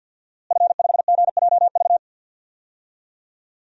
Morse code, heard as a keyed tone, about 49 wpm, 700 Hz. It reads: F5GJV
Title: DXCC-100 Call Signs - 5 Characters - Repeated 3x - 50wpm